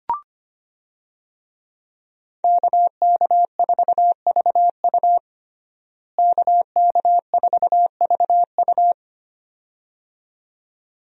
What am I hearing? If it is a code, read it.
Morse code, heard as a keyed tone, about 25 wpm, 690 Hz. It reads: KK4VU KK4VU
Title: USA Call Signs - 5 Characters - Repeated 2x - 25wpm